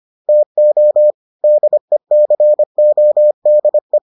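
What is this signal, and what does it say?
Morse code, heard as a keyed tone, about 25 wpm, 600 Hz. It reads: TO DECODE
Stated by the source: Morse edition of the Foundations of Amateur Radio episodes.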